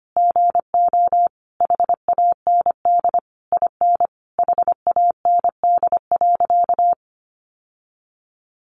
Morse code, heard as a keyed tone, about 25 wpm, 700 Hz. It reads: GO HAND IN HAND.
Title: Sentences from Top 1200 Words 25wpm